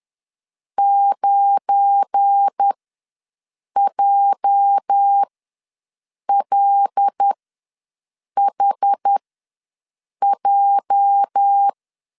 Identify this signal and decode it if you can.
Morse code, heard as a keyed tone, about 11 wpm, spaced as for 6 wpm, 800 Hz. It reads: 9JLHJ
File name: LEZIONI / 8 YJ45 / 8 YJ ws 10wpm 4043